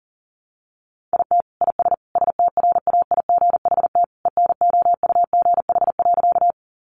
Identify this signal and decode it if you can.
Morse code, heard as a keyed tone, about 40 wpm, 700 Hz. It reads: IT IS STRAIGHT ENOUGH.
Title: Sentences from Top 1000 Words 40wpm